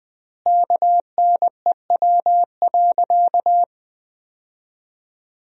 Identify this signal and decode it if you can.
Morse code, heard as a keyed tone, about 21 wpm, 700 Hz. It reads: KNEW.
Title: 4x Word-Spacing - Sentences from Top 400 Words - 20wpm